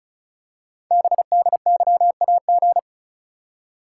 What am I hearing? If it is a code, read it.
Morse code, heard as a keyed tone, about 36 wpm, 700 Hz. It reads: BDYAG
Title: Code Groups - 5 Letters - 35wpm